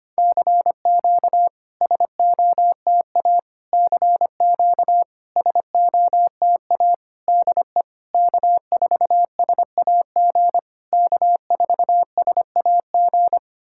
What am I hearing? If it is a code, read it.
Morse code, heard as a keyed tone, about 25 wpm, 700 Hz. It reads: CQ SOTA CQ SOTA DE K4SAG K4SAG
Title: SOTA QSO Practice 25wpm